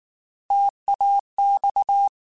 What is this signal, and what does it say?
Morse code, heard as a keyed tone, about 19 wpm, 800 Hz. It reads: TAX
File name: Alphabetical / Alphabetical mp3s / Ts @18